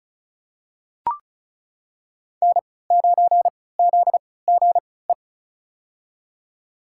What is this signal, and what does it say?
Morse code, heard as a keyed tone, about 36 wpm, spaced as for 20 wpm, 700 Hz. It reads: N9ZGE
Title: USA Call Signs - 5 Characters - 3x Character Spacing - 35wpm